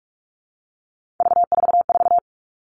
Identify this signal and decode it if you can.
Morse code, heard as a keyed tone, about 44 wpm, 700 Hz. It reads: V44
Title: All Cut Numbers - 3-Digit 45wpm